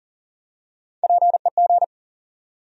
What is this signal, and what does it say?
Morse code, heard as a keyed tone, about 40 wpm, 700 Hz. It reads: PEG
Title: USA Common Names - Max 3 Letters - 40wpm